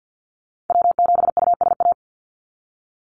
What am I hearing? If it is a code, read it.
Morse code, heard as a keyed tone, about 48 wpm, 700 Hz. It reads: W7USA
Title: NA SOTA Chaser Callsigns 50wpm